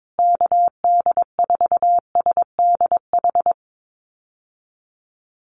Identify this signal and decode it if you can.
Morse code, heard as a keyed tone, about 22 wpm, 700 Hz. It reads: KD4SDH
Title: USA Call Signs Repeated 2x 22wpm